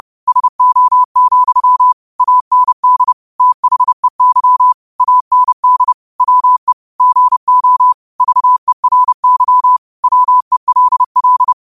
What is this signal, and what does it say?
Morse code, heard as a keyed tone, about 30 wpm, 1000 Hz. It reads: SO, AND THEY AND WE GO VERY WELL